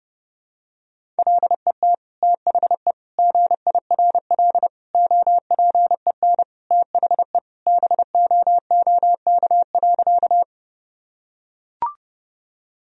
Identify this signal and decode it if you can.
Morse code, heard as a keyed tone, about 30 wpm, 700 Hz. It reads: LET THE GIRL OPEN THE BOOK.
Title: Sentences from Top 1000 Words - Review Part 1 - 30wpm